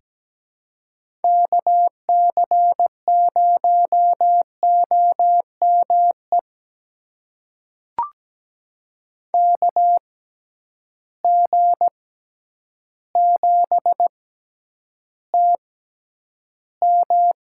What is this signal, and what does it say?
Morse code, heard as a keyed tone, about 17 wpm, spaced as for 9 wpm, 700 Hz. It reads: KC0OME KG7TM
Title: USA Call Signs - 6 Characters - 6x Character Spacing - 17wpm